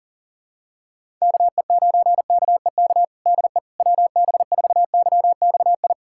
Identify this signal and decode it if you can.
Morse code, heard as a keyed tone, about 40 wpm, 700 Hz. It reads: KE9KEK DE WB4YXI
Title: USA Callsign DE Callsign - 6 Characters - 40wpm